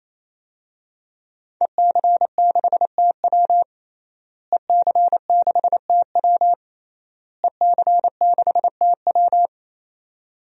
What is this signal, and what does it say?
Morse code, heard as a keyed tone, about 27 wpm, 700 Hz. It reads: EC6TW EC6TW EC6TW
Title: DXCC-100 Call Signs - 5 Characters - Repeated 3x - 28wpm